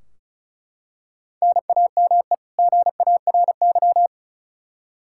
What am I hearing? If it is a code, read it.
Morse code, heard as a keyed tone, about 36 wpm, 700 Hz. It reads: NAME GARY
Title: Practice QSO - Exchange 1 - Beginning - 35wpm